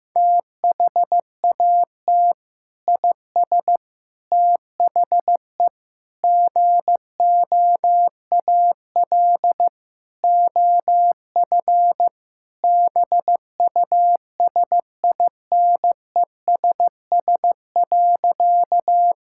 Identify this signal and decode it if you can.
Morse code, heard as a keyed tone, about 15 wpm, 700 Hz. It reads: THAT IS THE GOAL OF BUSINESS.
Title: Sentences from Top 1000 Words - Review Part 5 - 15wpm